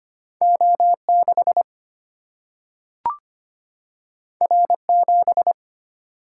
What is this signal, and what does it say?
Morse code, heard as a keyed tone, about 25 wpm, 700 Hz. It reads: O 6 R 7
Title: Single-Letter Single-Number 25wpm